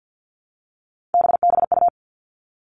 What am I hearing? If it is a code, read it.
Morse code, heard as a keyed tone, about 50 wpm, 700 Hz. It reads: BBU
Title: Three Challenging Letter-Number Combinations 50wpm